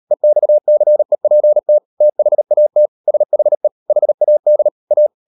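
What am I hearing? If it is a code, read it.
Morse code, heard as a keyed tone, about 38 wpm, 600 Hz. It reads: EXCEPT THAT SHE HAD A
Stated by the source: Morse code practice at 38 WPM, sending Ken Follett's novel Never in English, from CW med Gnister.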